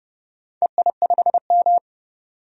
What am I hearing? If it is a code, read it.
Morse code, heard as a keyed tone, about 30 wpm, 700 Hz. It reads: EI5M
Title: DXCC-100 Call Signs - 4 Characters - 30wpm